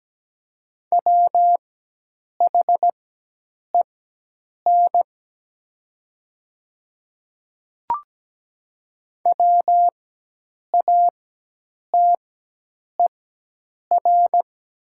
Morse code, heard as a keyed tone, about 17 wpm, spaced as for 8 wpm, 700 Hz. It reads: WHEN WATER
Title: 4x Character Spacing - Top 100 Words - 17wpm